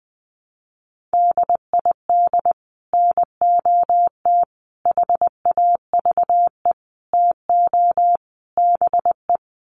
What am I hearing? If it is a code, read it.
Morse code, heard as a keyed tone, about 19 wpm, 700 Hz. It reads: DID NOT HAVE TO BE
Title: Sets of 5 Words Encore2 20wpm